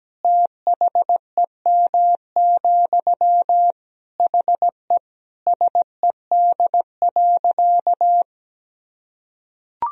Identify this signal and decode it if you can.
Morse code, heard as a keyed tone, about 17 wpm, 700 Hz. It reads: THEM, HE SED.
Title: Sentences from Top 400 Words and Abbreviations 17wpm